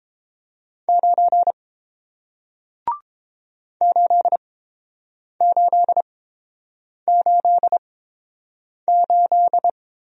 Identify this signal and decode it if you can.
Morse code, heard as a keyed tone, about 30 wpm, spaced as for 7 wpm, 700 Hz. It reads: 9 8888